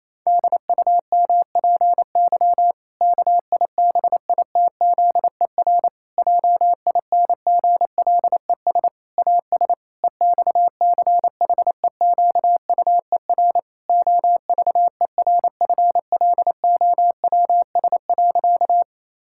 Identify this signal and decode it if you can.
Morse code, heard as a keyed tone, about 27 wpm, 700 Hz. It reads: DUMPY KIBITZER JINGLES AS EXCHEQUER OVERFLOWS.
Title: Pangram Challenge v2 - 28wpm